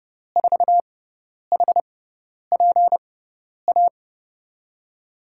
Morse code, heard as a keyed tone, about 30 wpm, spaced as for 10 wpm, 700 Hz. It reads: 4HPA
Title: USA Call Signs - 6 Characters - 6x Character Spacing - 30wpm